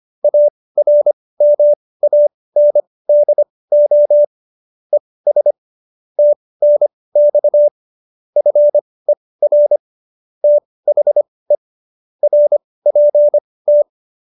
Morse code, heard as a keyed tone, about 25 wpm, spaced as for 18 wpm, 590 Hz. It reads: ARMANDO ES TNX FER THE RPT